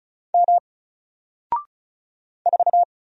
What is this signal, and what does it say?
Morse code, heard as a keyed tone, about 36 wpm, 700 Hz. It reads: M 4
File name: Single-Letter-Number-35wpm